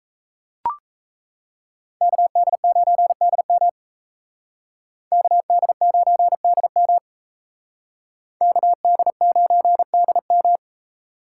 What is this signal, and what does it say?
Morse code, heard as a keyed tone, about 36 wpm, 700 Hz. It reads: KD9DM KD9DM KD9DM